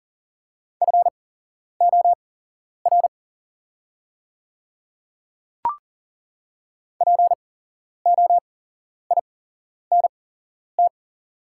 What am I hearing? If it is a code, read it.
Morse code, heard as a keyed tone, about 39 wpm, spaced as for 11 wpm, 690 Hz. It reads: FOR POINT
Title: 8x Character Spacing - Top 200 Words - 40wpm